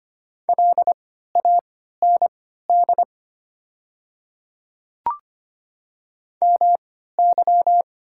Morse code, heard as a keyed tone, about 25 wpm, spaced as for 14 wpm, 690 Hz. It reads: LAND MY
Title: 3x Character Spacing - Top 300 Words - 25wpm